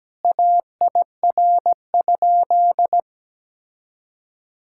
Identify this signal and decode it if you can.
Morse code, heard as a keyed tone, about 17 wpm, 700 Hz. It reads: AIR?
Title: Sentences from Top 1000 Words - Review Part 6 - 17wpm